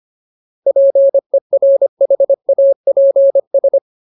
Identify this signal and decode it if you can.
Morse code, heard as a keyed tone, about 25 wpm, 550 Hz. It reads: PERHAPS
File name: Ultimate Word List / 25WPM Ultimate